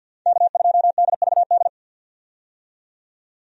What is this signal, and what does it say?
Morse code, heard as a keyed tone, about 51 wpm, 700 Hz. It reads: K2DVD
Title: USA Call Signs - 5 Characters - 50wpm